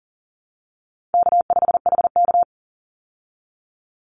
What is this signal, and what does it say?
Morse code, heard as a keyed tone, about 37 wpm, 700 Hz. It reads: K5HK
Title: Easy Call Signs Repeated 2x - 40wpm